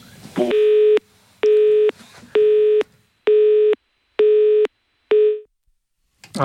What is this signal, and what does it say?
Busy tone (Europe, 425 Hz).